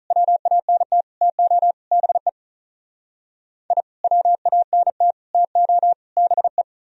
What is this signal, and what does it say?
Morse code, heard as a keyed tone, about 43 wpm, 700 Hz. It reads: WANT TO BE I WANT TO BE
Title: Speed-Racing - Sets of 4 Words 35wpm